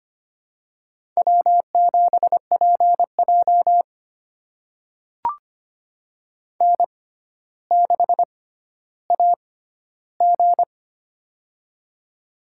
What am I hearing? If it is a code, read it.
Morse code, heard as a keyed tone, about 25 wpm, spaced as for 13 wpm, 700 Hz. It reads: W7PJ N6AG
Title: USA Call Signs - 4 Characters - 6x Character Spacing - 25wpm